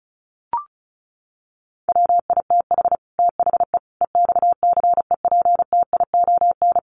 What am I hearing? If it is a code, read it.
Morse code, heard as a keyed tone, about 34 wpm, 700 Hz. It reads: WITH THE EXCEPTION
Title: Sets of 4 Words 35wpm